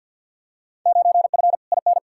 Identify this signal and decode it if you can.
Morse code, heard as a keyed tone, about 53 wpm, 700 Hz. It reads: 9F IN
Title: Field Day Exchanges Encore - 50wpm